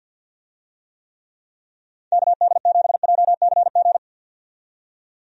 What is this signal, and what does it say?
Morse code, heard as a keyed tone, about 51 wpm, 700 Hz. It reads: KD7JCG
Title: USA Call Signs - 6 Characters - 50wpm